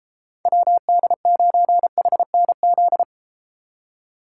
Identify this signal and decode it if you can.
Morse code, heard as a keyed tone, about 32 wpm, 700 Hz. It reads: WD9HNZ